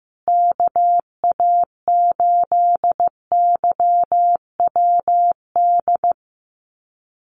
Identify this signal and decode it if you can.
Morse code, heard as a keyed tone, about 15 wpm, 700 Hz. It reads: KA8YWD
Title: USA Call Signs Repeated 2x 15wpm